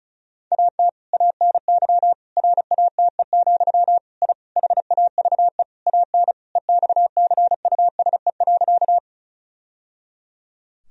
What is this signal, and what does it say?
Morse code, heard as a keyed tone, about 34 wpm, 700 Hz. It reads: AT ANY RATE, I HAVE AN EXCUSE.
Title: Sentences from Top 1000 Words - Review Part 4 - 35wpm